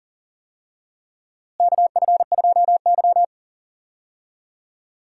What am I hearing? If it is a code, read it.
Morse code, heard as a keyed tone, about 40 wpm, 700 Hz. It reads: KF2Y
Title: USA Call Signs - 4 Characters - Repeated 2x - 40wpm